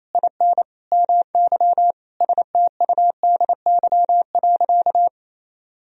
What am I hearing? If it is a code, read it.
Morse code, heard as a keyed tone, about 28 wpm, 700 Hz. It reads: IN MY STUDY.
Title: Sentences from Top 1000 Words - Review Part 1 - 28wpm